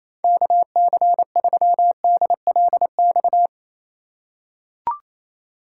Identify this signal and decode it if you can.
Morse code, heard as a keyed tone, about 28 wpm, 700 Hz. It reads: KC3DLX
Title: USA Call Signs - 6 Characters - 3x Character Spacing - 28wpm